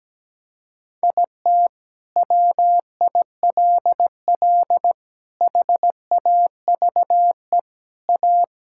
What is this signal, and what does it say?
Morse code, heard as a keyed tone, about 17 wpm, 700 Hz. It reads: IT WILL HAVE A
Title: Sentences from Top 1000 Words - Review Part 6 - 17wpm